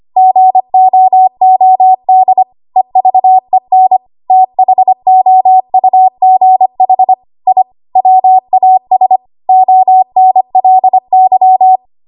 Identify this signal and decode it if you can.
Morse code, heard as a keyed tone, about 25 wpm, 750 Hz. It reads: GOOD EVEN THOUGH I WAS ONLY
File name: Archive / 25 WPM / 25WPM